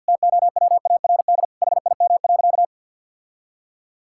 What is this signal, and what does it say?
Morse code, heard as a keyed tone, about 50 wpm, 700 Hz. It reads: TOWARD HIM.
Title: Sentences from Top 1000 Words - Review Part 6 - 50wpm